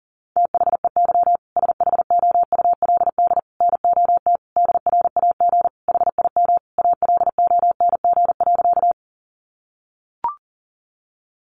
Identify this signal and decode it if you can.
Morse code, heard as a keyed tone, about 40 wpm, 700 Hz. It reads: THEY SHOULD NOT DRAG HIM ALONG.
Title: Sentences from Top 1500 Words 40wpm